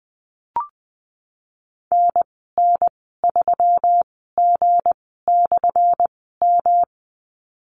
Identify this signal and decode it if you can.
Morse code, heard as a keyed tone, about 20 wpm, spaced as for 15 wpm, 700 Hz. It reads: NN3G/M
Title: 179 Lesson 29g - Slash - Call Signs 20wpm